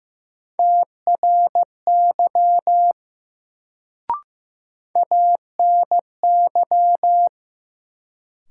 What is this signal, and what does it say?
Morse code, heard as a keyed tone, about 15 wpm, 700 Hz. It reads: TRY ANY
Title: Sentences from Top 200 Words - New Word Drill - 15wpm